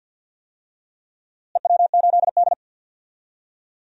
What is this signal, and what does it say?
Morse code, heard as a keyed tone, about 47 wpm, 700 Hz. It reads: EW8D
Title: DXCC-100 Call Signs - 4 Characters - Repeated 2x - 50wpm